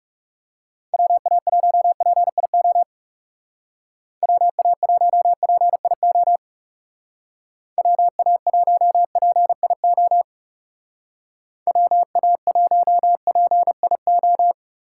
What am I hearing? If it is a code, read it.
Morse code, heard as a keyed tone, about 37 wpm, 700 Hz. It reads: WA1PIO WA1PIO WA1PIO WA1PIO
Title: Speed Racing - USA 2x3 Call Signs 30wpm